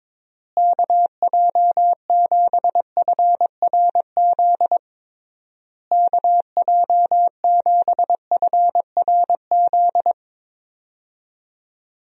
Morse code, heard as a keyed tone, about 22 wpm, 700 Hz. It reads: KJ7FRZ KJ7FRZ
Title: USA Call Signs - 6 Characters - Repeated 3x - 22wpm